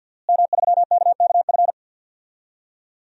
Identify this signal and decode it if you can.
Morse code, heard as a keyed tone, about 51 wpm, 700 Hz. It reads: M3KKF
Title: DXCC-100 Call Signs - 5 Characters - Repeated 3x - 50wpm